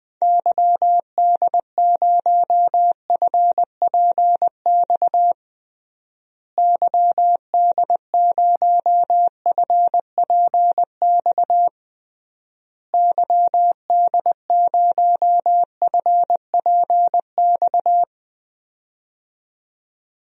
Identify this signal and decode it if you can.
Morse code, heard as a keyed tone, about 21 wpm, 700 Hz. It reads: YD0FPX YD0FPX YD0FPX
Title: DXCC-100 Call Signs - 6 Characters - Repeated 3x - 20wpm